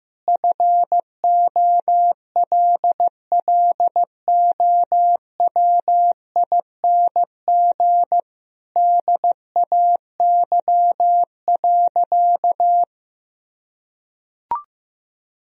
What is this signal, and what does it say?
Morse code, heard as a keyed tone, about 16 wpm, 700 Hz. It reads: FOLLOWING DAY.
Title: Sentences from Top 1000 Words - Review Part 4 - 15wpm